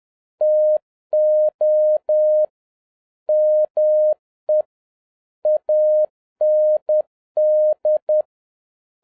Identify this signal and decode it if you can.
Morse code, heard as a keyed tone, about 10 wpm, 610 Hz. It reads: TO ME AND
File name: 2011-07-mp3 / n31 THE 3.11 2 10wpm